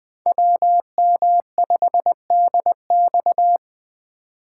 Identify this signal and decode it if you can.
Morse code, heard as a keyed tone, about 20 wpm, 700 Hz. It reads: WM5DX
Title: K1USN SST QSO Practice - 20wpm